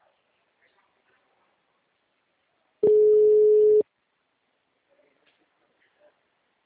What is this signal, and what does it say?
Ringback tone (Europe, 425 Hz).
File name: out- -2019- - - .71794